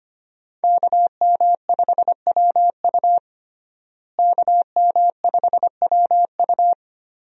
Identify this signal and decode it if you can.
Morse code, heard as a keyed tone, about 25 wpm, 700 Hz. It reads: KM5WU KM5WU
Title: USA Call Signs - 5 Characters - Repeated 2x - 25wpm